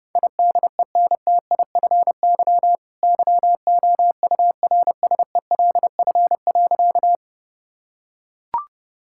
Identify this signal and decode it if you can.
Morse code, heard as a keyed tone, about 30 wpm, 700 Hz. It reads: IDENTIFY YOURSELF.
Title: Sentences from Top 1000 Words - Review Part 6 - 30wpm